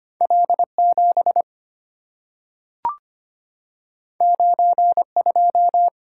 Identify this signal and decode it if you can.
Morse code, heard as a keyed tone, about 25 wpm, 690 Hz. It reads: L7 92
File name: Two Letter-Number Combinations - 25wpm